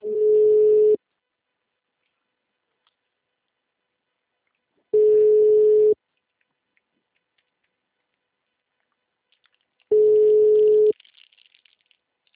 Ringback tone (Europe, 425 Hz).
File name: out- -2006- - - .177